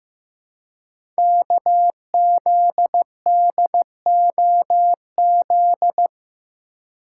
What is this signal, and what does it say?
Morse code, heard as a keyed tone, about 15 wpm, 700 Hz. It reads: KZDOZ
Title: Code Groups - 5 Letter:Numbers - 15wpm